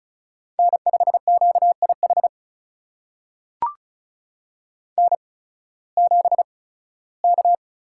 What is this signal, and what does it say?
Morse code, heard as a keyed tone, about 36 wpm, spaced as for 19 wpm, 700 Hz. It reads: N5QIH N7K
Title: USA Call Signs - 8x Character Spacing - 35wpm